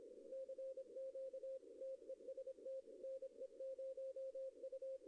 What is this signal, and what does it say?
Morse code, heard as a keyed tone, about 25 wpm, 530 Hz. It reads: CQ TEST NE0U
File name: contests / arrl-160-k5tr-op-w5tn -0100